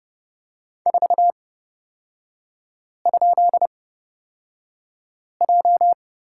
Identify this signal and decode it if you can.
Morse code, heard as a keyed tone, about 30 wpm, spaced as for 5 wpm, 700 Hz. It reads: 4?J
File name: Lesson 36e - BT - Three Characters 30wpm